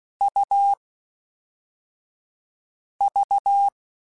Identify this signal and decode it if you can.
Morse code, heard as a keyed tone, about 16 wpm, 1350 Hz. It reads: U V
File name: Difficult Combos / Difficult Combos-mp3s / AUV AUV4 @5